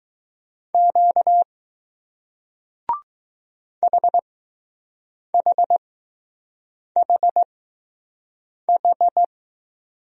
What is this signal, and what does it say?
Morse code, heard as a keyed tone, about 20 wpm, spaced as for 6 wpm, 700 Hz. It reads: Q HHHH